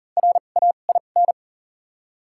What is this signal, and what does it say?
Morse code, heard as a keyed tone, about 37 wpm, 700 Hz. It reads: R A I N
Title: 2x Character Spacing - Top 500 Words - 40wpm